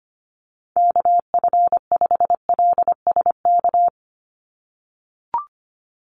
Morse code, heard as a keyed tone, about 25 wpm, 700 Hz. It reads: KF5LSK
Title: USA Call Signs 25wpm